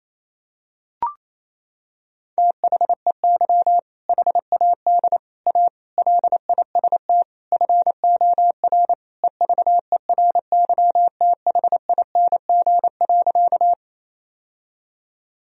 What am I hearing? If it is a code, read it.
Morse code, heard as a keyed tone, about 28 wpm, 700 Hz. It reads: THEY HAD A LIST FOR EVERYTHING.
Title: Sentences from Top 1000 Words - Review Part 2 - 28wpm